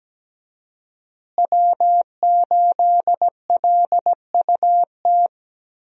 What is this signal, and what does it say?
Morse code, heard as a keyed tone, about 17 wpm, 700 Hz. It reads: W8LUT
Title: USA Call Signs - 5 Characters - Repeated 3x - 17wpm